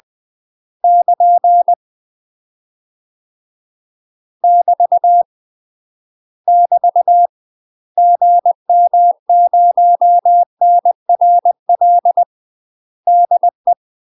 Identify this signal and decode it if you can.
Morse code, heard as a keyed tone, about 20 wpm, 700 Hz. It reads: <KN> <BT> <BT> GM0NRL DE
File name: UK MASTER QSOs TRAINING / 20WPM / UK MASTER QSOs 0022